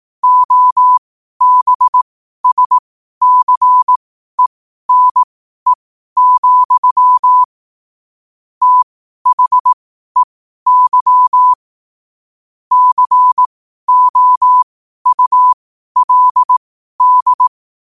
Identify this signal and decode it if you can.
Morse code, heard as a keyed tone, about 18 wpm, spaced as for 12 wpm, 1000 Hz. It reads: OBSCENE, THEY COULD
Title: 12 WPM morse code quotes for Fri, 01 Mar 2024